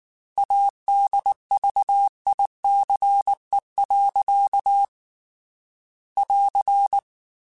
Morse code, heard as a keyed tone, about 19 wpm, 800 Hz. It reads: ADVICE. <AR>
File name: Quotes / Quotes-mp3s / Advice @18